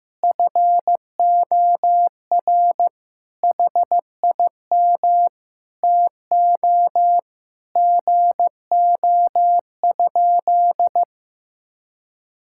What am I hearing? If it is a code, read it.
Morse code, heard as a keyed tone, about 15 wpm, 700 Hz. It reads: FOR HIM TO GO?
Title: Sentences from Top 1600 Words 15wpm